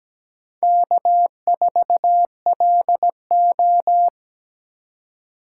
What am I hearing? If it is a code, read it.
Morse code, heard as a keyed tone, about 17 wpm, 700 Hz. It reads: K4LO
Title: USA Call Signs - 3x Character Spacing - 17wpm